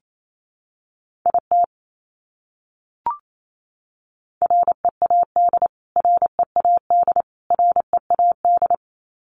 Morse code, heard as a keyed tone, about 28 wpm, 700 Hz. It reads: IT READ READ READ
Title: Top 200 Words Repeated 28wpm